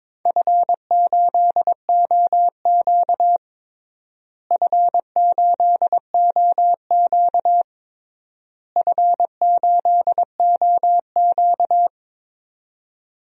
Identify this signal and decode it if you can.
Morse code, heard as a keyed tone, about 22 wpm, 700 Hz. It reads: F8OQ F8OQ F8OQ
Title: DXCC-100 Call Signs - 4 Characters - Repeated 3x - 22wpm